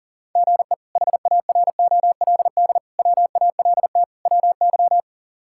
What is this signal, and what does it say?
Morse code, heard as a keyed tone, about 40 wpm, 700 Hz. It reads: GE HAROLD WALT WY
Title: K1USN SST QSO Practice - 40wpm